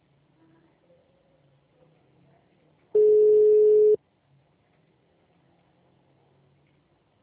Ringback tone (Europe, 425 Hz).